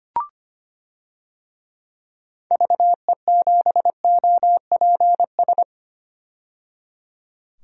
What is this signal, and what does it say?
Morse code, heard as a keyed tone, about 25 wpm, 700 Hz. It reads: VE7OPS
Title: DXCC-100 Call Signs - 6 Characters - 25wpm